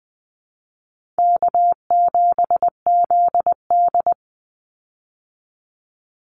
Morse code, heard as a keyed tone, about 20 wpm, 700 Hz. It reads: K7ZD
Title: Easy Call Signs Repeated 3x - 20wpm